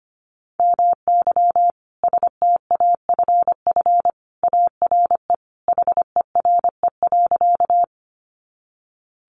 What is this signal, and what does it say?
Morse code, heard as a keyed tone, about 25 wpm, 700 Hz. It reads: MY STAFF ARE HERE.
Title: Sentences from Top 1200 Words 25wpm